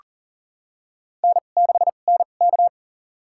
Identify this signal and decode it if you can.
Morse code, heard as a keyed tone, about 42 wpm, spaced as for 29 wpm, 690 Hz. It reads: N6NK
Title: USA Call Signs - 4 Characters - 2x Character Spacing - 40wpm